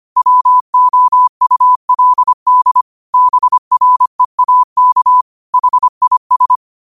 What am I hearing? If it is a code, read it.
Morse code, heard as a keyed tone, about 25 wpm, 1000 Hz. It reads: WOULD BREAK HIS